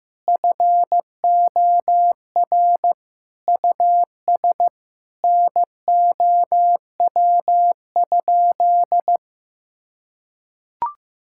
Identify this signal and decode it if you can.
Morse code, heard as a keyed tone, about 15 wpm, 700 Hz. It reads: FOR US NOW?
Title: Sentences from Top 1000 Words - Review Part 5 - 15wpm